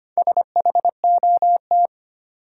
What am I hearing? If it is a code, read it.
Morse code, heard as a keyed tone, about 25 wpm, 700 Hz. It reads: SHOT
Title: Sentences from Top 1100 Words - New Word Drill Alt - 25wpm